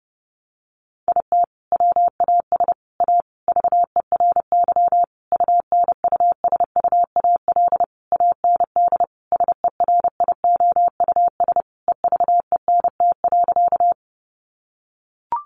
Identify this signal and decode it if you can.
Morse code, heard as a keyed tone, about 30 wpm, 700 Hz. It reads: IT WAS A VERY UNUSUAL AND SERIOUS EVENT.
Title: Sentences from Top 1400 Words 30wpm